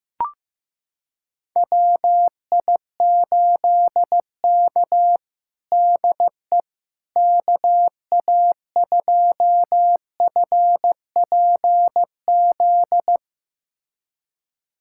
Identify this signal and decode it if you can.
Morse code, heard as a keyed tone, about 15 wpm, 700 Hz. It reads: WI8K DE KA2FPZ
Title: USA Callsign DE Callsign 15wpm